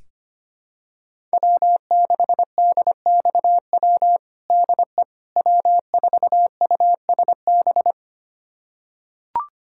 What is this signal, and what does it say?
Morse code, heard as a keyed tone, about 25 wpm, 700 Hz. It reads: W6DXW DE W4USB
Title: USA Callsign DE Callsign - 5 Characters - 25wpm